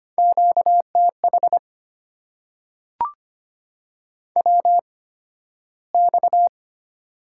Morse code, heard as a keyed tone, about 25 wpm, spaced as for 13 wpm, 690 Hz. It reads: QTH WX
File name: Top 90 QSO Elements - 8x Character Spacing - 25wpm